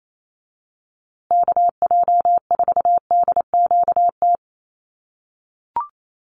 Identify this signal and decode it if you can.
Morse code, heard as a keyed tone, about 28 wpm, 700 Hz. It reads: KJ4DQT
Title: USA Call Signs Repeated 3x 28wpm